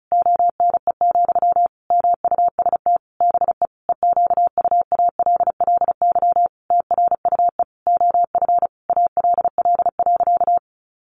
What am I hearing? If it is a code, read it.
Morse code, heard as a keyed tone, about 34 wpm, 700 Hz. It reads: ONE, MUST BE EQUALLY TRUE OF ALL.
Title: Sentences from Top 1400 Words 35wpm